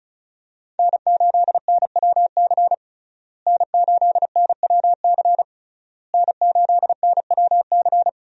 Morse code, heard as a keyed tone, about 36 wpm, 700 Hz. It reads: N8NWC N8NWC N8NWC
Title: USA Call Signs - 5 Characters - Repeated 3x - 35wpm